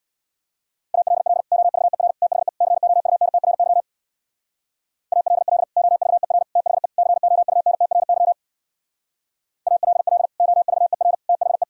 Morse code, heard as a keyed tone, about 62 wpm, 700 Hz. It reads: ALL OVER THE COUNTRY ALL OVER THE COUNTRY ALL OVER THE
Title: Speed-Racing - Sets of 4 Words 50wpm